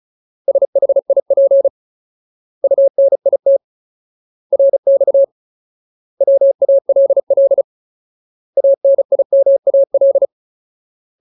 Morse code, heard as a keyed tone, about 34 wpm, 550 Hz. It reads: SHIP UNIT RX WALL ANIMAL
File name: Ultimate Word List / 35WPM Ultimate